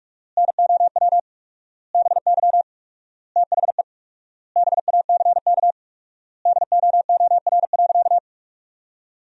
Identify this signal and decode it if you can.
Morse code, heard as a keyed tone, about 42 wpm, 700 Hz. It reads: NOW BY THE BACK DOOR.
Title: 4x Word-Spacing - Sentences from Top 300 Words - 45wpm